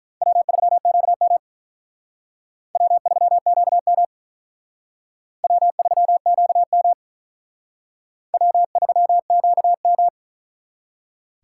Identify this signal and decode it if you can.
Morse code, heard as a keyed tone, about 45 wpm, 700 Hz. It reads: W3QM W3QM W3QM W3QM